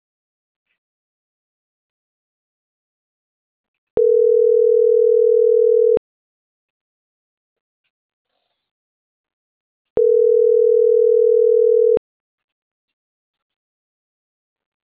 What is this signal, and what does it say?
Ringback tone (North America), 440 + 480 Hz.